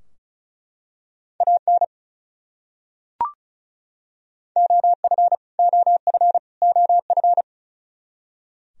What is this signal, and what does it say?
Morse code, heard as a keyed tone, about 34 wpm, 700 Hz. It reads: AN OF OF OF
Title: Top 100 Words Repeated 35wpm